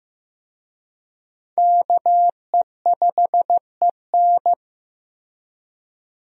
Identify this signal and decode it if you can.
Morse code, heard as a keyed tone, about 15 wpm, 700 Hz. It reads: KE5EN
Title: USA Call Signs - 5 Characters - Repeated 2x - 15wpm